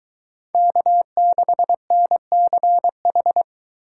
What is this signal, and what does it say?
Morse code, heard as a keyed tone, about 23 wpm, 700 Hz. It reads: K6NCH